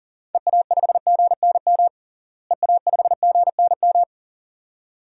Morse code, heard as a keyed tone, about 40 wpm, 700 Hz. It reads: EA5GNM EA5GNM
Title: DXCC-100 Call Signs - 6 Characters - Repeated 2x - 40wpm